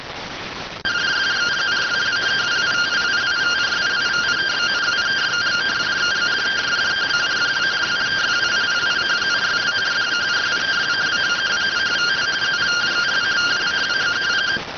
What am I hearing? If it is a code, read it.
Two-tone teleprinter signal, about 170 Hz shift.